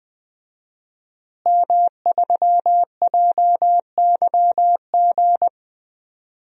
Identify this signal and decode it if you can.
Morse code, heard as a keyed tone, about 20 wpm, 700 Hz. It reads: M3JYG
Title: DXCC-100 Call Signs - 5 Characters - Repeated 3x - 20wpm